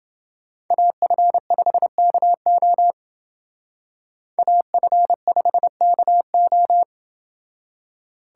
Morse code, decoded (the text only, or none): AF5KO AF5KO